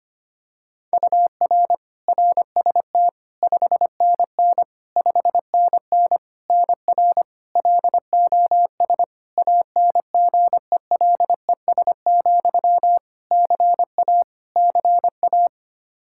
Morse code, heard as a keyed tone, about 25 wpm, 700 Hz. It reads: UR RST 5NN 5NN NR LOS ANGELES, CA CA
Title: K1USN SST QSO Practice 25wpm